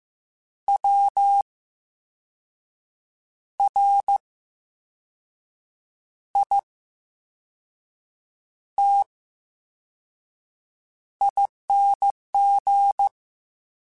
Morse code, heard as a keyed tone, about 15 wpm, spaced as for 5 wpm, 800 Hz. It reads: WRITING
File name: Special INGs-mp3s / Special INGs-mp3s / VerySlowToFaster Ing @13